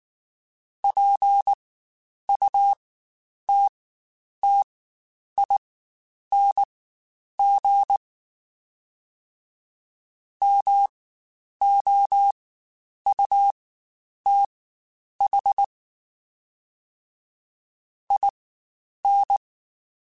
Morse code, heard as a keyed tone, about 19 wpm, spaced as for 9 wpm, 800 Hz. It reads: PUTTING MOUTH IN
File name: Spaced / Chuckles Spaced-mp3s / BrainMouth sp @18